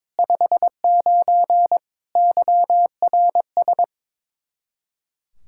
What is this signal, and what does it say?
Morse code, heard as a keyed tone, about 22 wpm, 700 Hz. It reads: 59 YRS
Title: Practice QSO - Exchange 3 - More Info - 22wpm